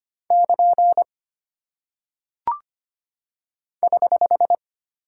Morse code, heard as a keyed tone, about 25 wpm, 700 Hz. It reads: <KN> <HH>
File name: Single Punctuation-Prosign - 25wpm